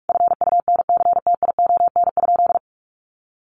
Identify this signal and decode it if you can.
Morse code, heard as a keyed tone, about 44 wpm, 700 Hz. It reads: FUNCTION?
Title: Sentences from Top 1100 Words 45wpm